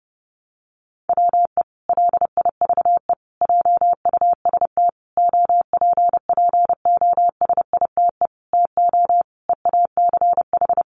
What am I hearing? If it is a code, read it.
Morse code, heard as a keyed tone, about 30 wpm, 700 Hz. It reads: WE LIVE JUST OPPOSITE TO EACH
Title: Sentences from Top 1300 Words 30wpm